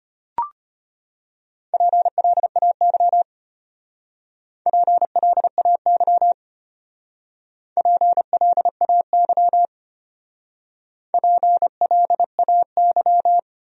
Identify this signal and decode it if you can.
Morse code, heard as a keyed tone, about 32 wpm, 700 Hz. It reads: PLAY PLAY PLAY PLAY